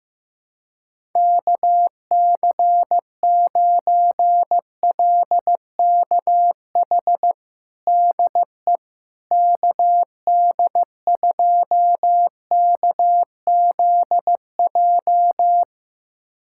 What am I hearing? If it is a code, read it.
Morse code, heard as a keyed tone, about 16 wpm, 700 Hz. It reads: KC9LKH DE KD2KZJ
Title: USA Callsign DE Callsign 15wpm